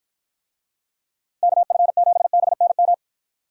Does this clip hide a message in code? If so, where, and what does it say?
Yes: Morse code from 1.4 to 2.9 s, heard as a keyed tone, about 50 wpm, 700 Hz. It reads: KF7BNM